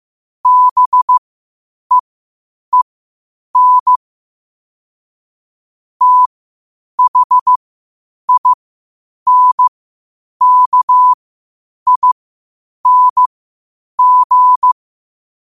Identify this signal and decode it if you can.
Morse code, heard as a keyed tone, about 15 wpm, spaced as for 8 wpm, 1000 Hz. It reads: BEEN THINKING
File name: qotdfiles / 8wpm